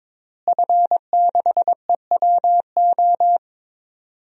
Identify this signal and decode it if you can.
Morse code, heard as a keyed tone, about 22 wpm, 700 Hz. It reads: F6EWO
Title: DXCC-100 Call Signs - 5 Characters - Repeated 3x - 22wpm